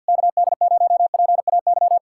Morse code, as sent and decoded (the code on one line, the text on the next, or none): -.- -.. ----- .--. .- -.--
KD0PAY